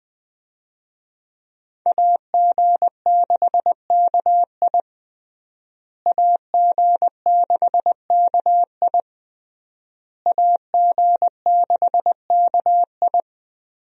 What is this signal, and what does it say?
Morse code, heard as a keyed tone, about 20 wpm, 700 Hz. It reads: AG6KI AG6KI AG6KI
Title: USA Call Signs - 5 Characters - Repeated 3x - 20wpm